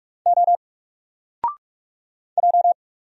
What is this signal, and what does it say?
Morse code, heard as a keyed tone, about 44 wpm, 700 Hz. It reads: O J
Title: Single Character (Letter:Number:Punctuation:Prosign) 45wpm